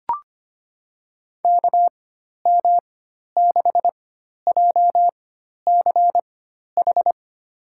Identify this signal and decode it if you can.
Morse code, heard as a keyed tone, about 25 wpm, spaced as for 12 wpm, 700 Hz. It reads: KM6JCH
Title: USA Call Signs - 6 Characters - 4x Character Spacing - 25wpm